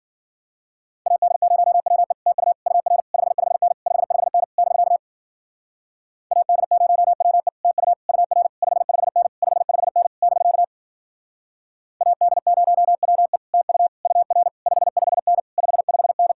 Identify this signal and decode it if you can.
Morse code, heard as a keyed tone, about 52 wpm, 700 Hz. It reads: AD0WE TU UR 55N 55N <BK> AD0WE TU UR 55N 55N <BK> AD0WE TU UR 55N 55N